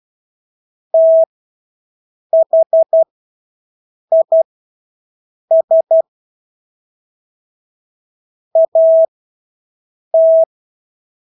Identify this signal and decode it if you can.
Morse code, heard as a keyed tone, about 12 wpm, spaced as for 6 wpm, 650 Hz. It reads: THIS AT
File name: Engelske ord 12-6 wpm